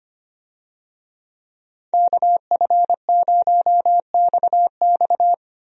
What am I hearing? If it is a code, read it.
Morse code, heard as a keyed tone, about 25 wpm, 700 Hz. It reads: KF0XX
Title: USA Call Signs - 5 Characters - Repeated 2x - 25wpm